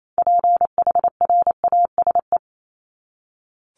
Morse code, heard as a keyed tone, about 28 wpm, 700 Hz. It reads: PHRASE
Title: Top 1501-1750 Words 28wpm